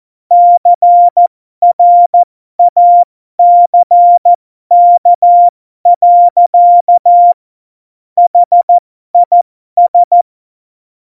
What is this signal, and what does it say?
Morse code, heard as a keyed tone, about 14 wpm, spaced as for 12 wpm, 700 Hz. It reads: CRACK. HIS
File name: Morsekurs / Witze